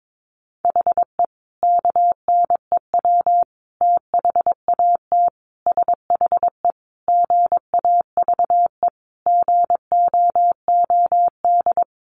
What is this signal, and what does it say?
Morse code, heard as a keyed tone, about 22 wpm, 700 Hz. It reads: HE KNEW THAT SHE GAVE GOOD
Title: Sentences from Top 1100 Words 22wpm